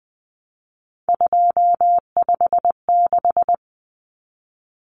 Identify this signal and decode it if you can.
Morse code, heard as a keyed tone, about 20 wpm, 700 Hz. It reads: 256
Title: Common Cut Numbers - 3-Digit 20wpm